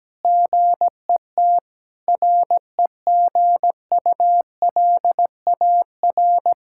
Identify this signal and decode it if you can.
Morse code, heard as a keyed tone, about 17 wpm, 700 Hz. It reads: GET REGULAR
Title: Sentences from Top 1000 Words - Review Part 6 - 17wpm